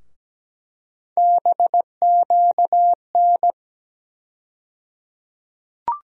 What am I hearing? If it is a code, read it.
Morse code, heard as a keyed tone, about 17 wpm, 700 Hz. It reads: BQN
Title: USA Call Sign Postfixes - 3 Letters 17wpm